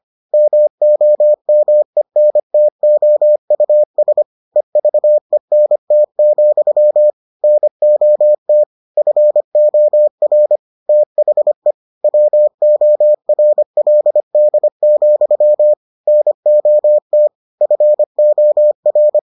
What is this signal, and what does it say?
Morse code, heard as a keyed tone, about 25 wpm, 600 Hz. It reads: MOMENTOUS EVENT, NOT FOR THE WORLD, NOT FOR